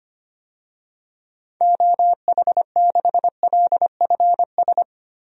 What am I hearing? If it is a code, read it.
Morse code, heard as a keyed tone, about 25 wpm, 700 Hz. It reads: OH6LFS
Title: DXCC-100 Call Signs - 6 Characters - 25wpm